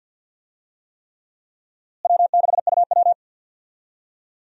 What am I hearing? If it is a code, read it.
Morse code, heard as a keyed tone, about 49 wpm, 700 Hz. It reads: W6UW
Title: USA Call Signs - 4 Characters - Repeated 3x - 50wpm